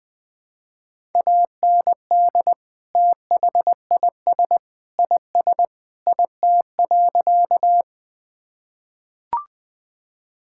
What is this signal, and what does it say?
Morse code, heard as a keyed tone, about 20 wpm, 700 Hz. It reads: AND THIS IS IT.
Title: Sentences with 3-Word Phrases - Spoken Beginning - 20wpm